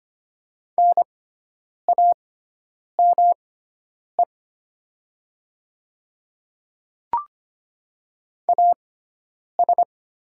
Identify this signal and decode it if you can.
Morse code, heard as a keyed tone, about 25 wpm, spaced as for 9 wpm, 700 Hz. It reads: NAME AS
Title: 6x Character Spacing - Top 300 Words - 25wpm